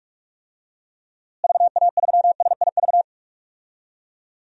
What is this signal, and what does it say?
Morse code, heard as a keyed tone, about 42 wpm, 700 Hz. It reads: VA3SIV
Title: DXCC-100 Call Signs - 6 Characters - Repeated 2x - 45wpm